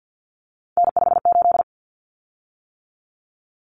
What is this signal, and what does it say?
Morse code, heard as a keyed tone, about 50 wpm, 700 Hz. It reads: N58
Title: Common Cut Numbers - 3-Digit 50wpm